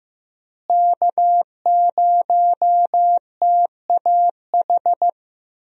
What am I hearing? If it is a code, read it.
Morse code, heard as a keyed tone, about 15 wpm, 700 Hz. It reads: K0TAH
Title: USA Call Signs - 5 Characters - Repeated 3x - 15wpm